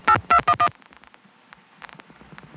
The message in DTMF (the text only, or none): 8211